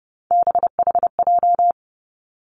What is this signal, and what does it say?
Morse code, heard as a keyed tone, about 30 wpm, 700 Hz. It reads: BHJ
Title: Three Challenging Letter-Number Combinations 30wpm